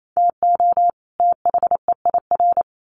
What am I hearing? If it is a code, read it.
Morse code, heard as a keyed tone, about 28 wpm, 700 Hz. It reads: TO THEIR
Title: Sets of 2 Words 28wpm